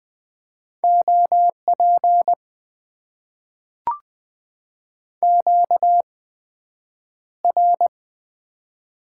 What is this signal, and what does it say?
Morse code, heard as a keyed tone, about 20 wpm, spaced as for 9 wpm, 700 Hz. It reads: OP QR
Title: Top 90 QSO Elements - 8x Character Spacing - 20wpm